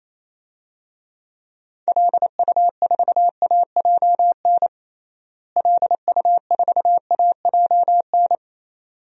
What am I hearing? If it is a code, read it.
Morse code, heard as a keyed tone, about 28 wpm, 700 Hz. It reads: LU4AJN LU4AJN
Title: DXCC-100 Call Signs - 6 Characters - Repeated 2x - 28wpm